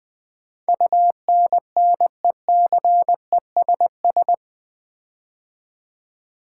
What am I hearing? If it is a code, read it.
Morse code, heard as a keyed tone, about 20 wpm, 700 Hz. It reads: UNNECESS
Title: Build A Word - 4 Letters At A Time - 11-Letter Words - 20wpm